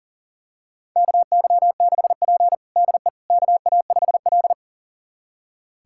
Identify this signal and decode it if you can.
Morse code, heard as a keyed tone, about 40 wpm, 700 Hz. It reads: KY6P DE KA5L
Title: USA Callsign DE Callsign - 4 Characters - 40wpm